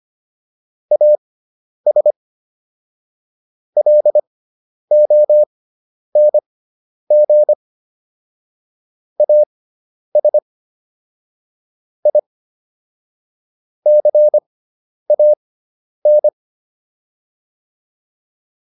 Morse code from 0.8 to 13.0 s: AS LONG AS I

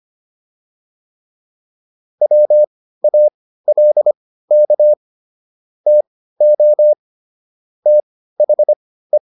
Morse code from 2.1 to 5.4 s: WALK

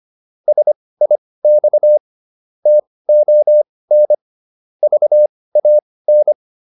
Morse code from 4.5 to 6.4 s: VAN

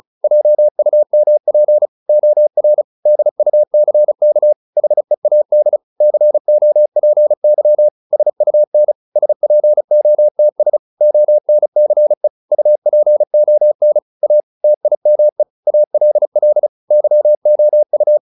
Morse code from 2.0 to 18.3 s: OR DUCK HEAD COPY SUN SPOTS ONCE UPON A TIME ALL YOU

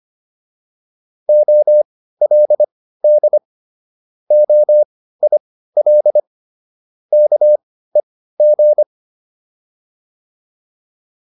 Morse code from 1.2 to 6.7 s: OLD OIL

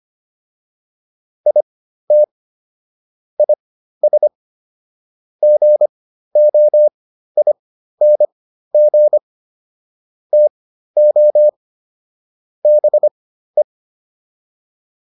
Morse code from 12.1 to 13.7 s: BE

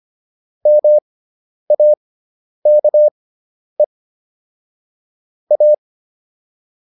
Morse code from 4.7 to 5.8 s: A

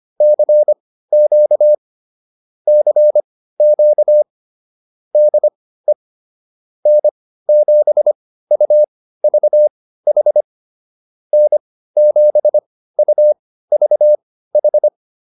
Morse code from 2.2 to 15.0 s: CQ DE N7UVH N7UVH

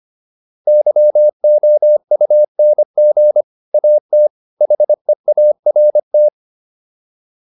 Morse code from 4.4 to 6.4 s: HEART

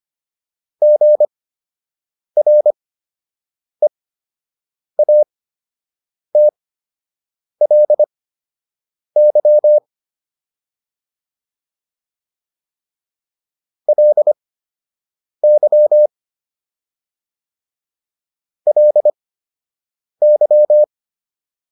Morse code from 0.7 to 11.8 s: GREATLY

